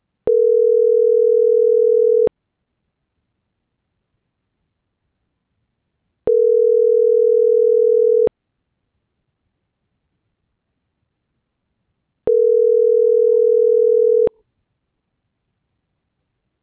Ringback tone (North America), 440 + 480 Hz.